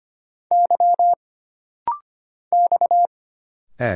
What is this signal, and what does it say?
Morse code, heard as a keyed tone, about 25 wpm, 700 Hz. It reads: Y X E
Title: Single Letters - Rapid-Fire 25wpm